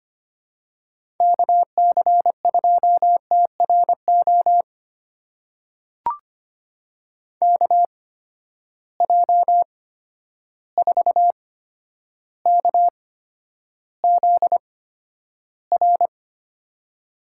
Morse code, heard as a keyed tone, about 25 wpm, spaced as for 11 wpm, 700 Hz. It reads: KC2TRO KJ4KZR
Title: USA Call Signs - 6 Characters - 8x Character Spacing - 25wpm